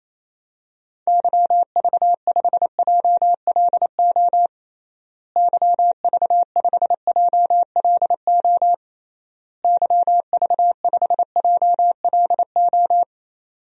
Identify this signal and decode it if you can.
Morse code, heard as a keyed tone, about 28 wpm, 700 Hz. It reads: YV5JLO YV5JLO YV5JLO
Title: DXCC-100 Call Signs - 6 Characters - Repeated 3x - 28wpm